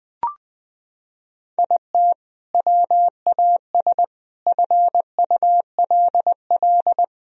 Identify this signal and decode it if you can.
Morse code, heard as a keyed tone, about 20 wpm, 700 Hz. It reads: IT WAS FULL
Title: Sentences from Top 1000 Words - Review Part 3 - 20wpm